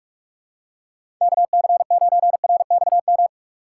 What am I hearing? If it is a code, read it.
Morse code, heard as a keyed tone, about 46 wpm, 700 Hz. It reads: KC9RXM
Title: USA Call Signs - 6 Characters - 4x Character Spacing - 45wpm